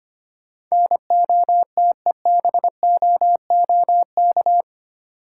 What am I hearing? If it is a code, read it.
Morse code, heard as a keyed tone, about 25 wpm, 690 Hz. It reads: NOTEBOOK
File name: practice / 25wpm / Easy Words with Hints - 25wpm